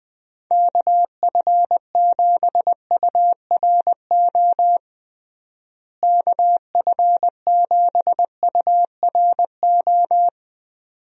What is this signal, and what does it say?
Morse code, heard as a keyed tone, about 21 wpm, 700 Hz. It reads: KF7URO KF7URO
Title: USA Call Signs - 6 Characters - Repeated 2x - 20wpm